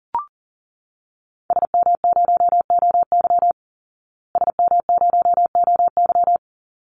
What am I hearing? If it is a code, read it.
Morse code, heard as a keyed tone, about 40 wpm, 700 Hz. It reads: SM0OY SM0OY
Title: Call Signs Repeated 3x - 40wpm